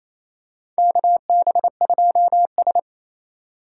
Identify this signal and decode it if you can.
Morse code, heard as a keyed tone, about 28 wpm, 700 Hz. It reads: KB2S
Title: K1USN SST QSO Practice - 28wpm